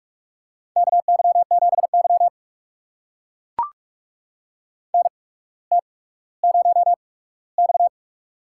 Morse code, heard as a keyed tone, about 44 wpm, spaced as for 19 wpm, 700 Hz. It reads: KY7Y NT0X
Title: USA Call Signs - 8x Character Spacing - 45wpm